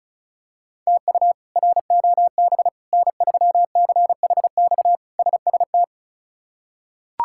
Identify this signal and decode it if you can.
Morse code, heard as a keyed tone, about 36 wpm, 700 Hz. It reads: TU ROB N3CHX SST
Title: K1USN SST QSO Practice - 35wpm